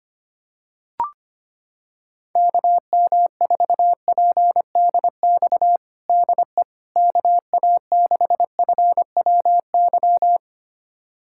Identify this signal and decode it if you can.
Morse code, heard as a keyed tone, about 25 wpm, 700 Hz. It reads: KM4PDX DE KA6FWY
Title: USA Callsign DE Callsign - 6 Characters - 25wpm